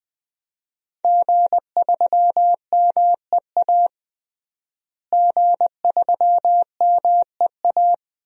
Morse code, heard as a keyed tone, about 20 wpm, 700 Hz. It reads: G3MEA G3MEA
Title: DXCC-100 Call Signs - 5 Characters - Repeated 2x - 20wpm